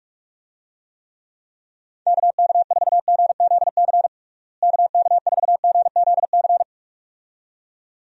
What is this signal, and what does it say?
Morse code, heard as a keyed tone, about 46 wpm, 700 Hz. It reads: KK4GZC KK4GZC
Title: USA Call Signs - 6 Characters - Repeated 2x - 45wpm